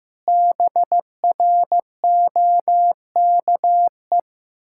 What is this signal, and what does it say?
Morse code, heard as a keyed tone, about 14 wpm, 700 Hz. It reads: BROKE
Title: Pangram Challenge v2 - 15wpm